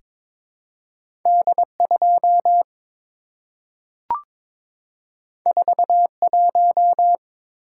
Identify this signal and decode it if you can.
Morse code, heard as a keyed tone, about 22 wpm, 700 Hz. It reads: D2 41
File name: practice / 22wpm / Two Letter-Number Combinations - 22wpm